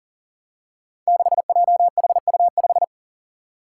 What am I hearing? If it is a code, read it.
Morse code, heard as a keyed tone, about 40 wpm, 700 Hz. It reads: 6JHU5
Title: Five Challenging Letter-Number Combinations 40wpm